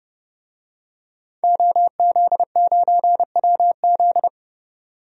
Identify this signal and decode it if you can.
Morse code, heard as a keyed tone, about 32 wpm, 700 Hz. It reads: OZ9WZ
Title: DXCC-100 Call Signs - 5 Characters - 30wpm